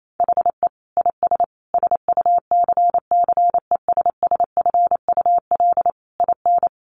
Morse code, heard as a keyed tone, about 28 wpm, 700 Hz. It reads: HE IS SUCCESSFUL IN
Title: Sentences from Top 900 Words 28wpm